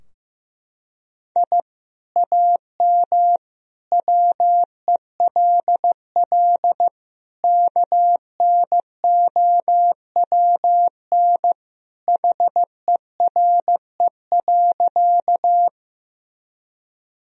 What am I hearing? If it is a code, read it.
Morse code, heard as a keyed tone, about 16 wpm, 700 Hz. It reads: I AM WELL KNOWN HERE.
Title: Sentences from Top 1000 Words - Review Part 3 - 15wpm